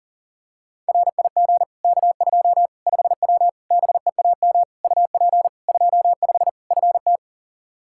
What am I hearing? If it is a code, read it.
Morse code, heard as a keyed tone, about 40 wpm, 700 Hz. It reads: RIG K2 5W BEAM UP 25 FT
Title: Practice QSO - Exchange 2 - Info - 40wpm